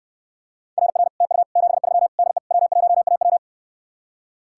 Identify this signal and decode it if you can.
Morse code, heard as a keyed tone, about 69 wpm, 700 Hz. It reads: RR TU 73 DE W1AW